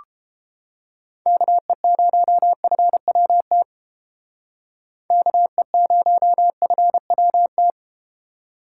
Morse code, heard as a keyed tone, about 30 wpm, 700 Hz. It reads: KE0FWT KE0FWT